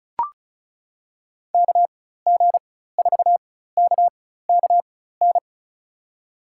Morse code, heard as a keyed tone, about 36 wpm, spaced as for 16 wpm, 700 Hz. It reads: KG4KKN
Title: USA Call Signs - 6 Characters - 4x Character Spacing - 35wpm